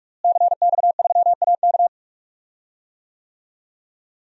Morse code, heard as a keyed tone, about 43 wpm, 700 Hz. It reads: CX3AK
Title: DXCC-100 Call Signs - 5 Characters - Repeated 2x - 45wpm